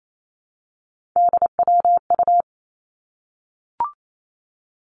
Morse code, heard as a keyed tone, about 28 wpm, 700 Hz. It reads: DWU
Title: Three Challenging Letter-Number Combinations 28wpm